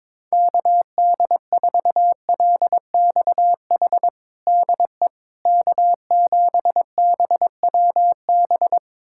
Morse code, heard as a keyed tone, about 22 wpm, 700 Hz. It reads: KD4LXH DE K7BWB
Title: USA Callsign DE Callsign 22wpm